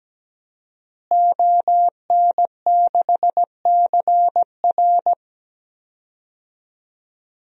Morse code, heard as a keyed tone, about 17 wpm, 700 Hz. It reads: ON6CR
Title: DXCC-100 Call Signs - 5 Characters - Repeated 2x - 17wpm